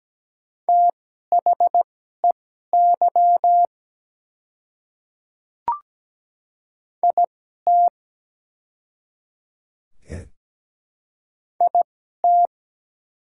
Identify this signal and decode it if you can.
Morse code, heard as a keyed tone, about 17 wpm, spaced as for 12 wpm, 700 Hz. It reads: THEY IT IT
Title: 2x Character Spacing - Top 100 Words - 17wpm